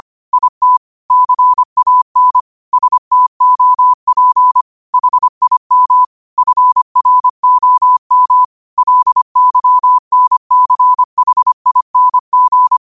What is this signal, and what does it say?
Morse code, heard as a keyed tone, about 25 wpm, 1000 Hz. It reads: IT CAN STOP HIM FROM LYNCHING
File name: qotdfiles / 25wpm